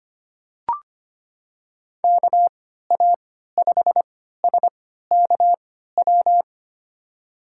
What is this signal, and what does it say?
Morse code, heard as a keyed tone, about 25 wpm, spaced as for 14 wpm, 700 Hz. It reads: KA5SKW
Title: USA Call Signs - 6 Characters - 3x Character Spacing - 25wpm